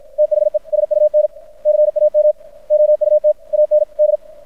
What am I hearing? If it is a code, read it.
Morse code, heard as a keyed tone, about 53 wpm, 600 Hz. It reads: THE SUN CAM OUT AND
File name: archived / CRSnet